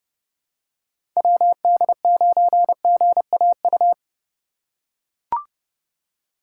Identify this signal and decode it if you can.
Morse code, heard as a keyed tone, about 30 wpm, 700 Hz. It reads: WD9GAU
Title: USA Call Signs - 6 Characters - 2x Character Spacing - 30wpm